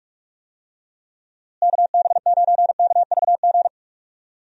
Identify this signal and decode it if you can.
Morse code, heard as a keyed tone, about 46 wpm, 700 Hz. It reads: KB9KVG
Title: USA Call Signs - 6 Characters - Repeated 3x - 45wpm